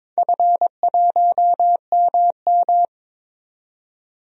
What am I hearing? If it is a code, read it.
Morse code, heard as a keyed tone, about 21 wpm, 700 Hz. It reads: F1MM
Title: DXCC-100 Call Signs - 4 Characters - 22wpm